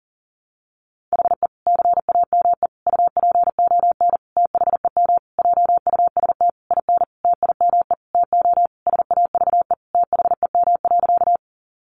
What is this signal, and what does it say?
Morse code, heard as a keyed tone, about 40 wpm, 700 Hz. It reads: HE CAME UPON THEM JUST IN TIME TO SAVE THEM.
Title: Sentences from Top 1100 Words 40wpm